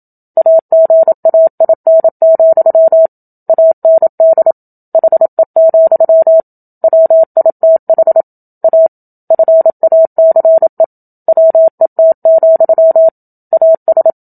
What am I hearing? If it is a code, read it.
Morse code, heard as a keyed tone, about 27 wpm, 650 Hz. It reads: AGAIN, AND HE, WITH A FACE WET, AS
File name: Book Two / 01-Chapter 1 - 27 WPM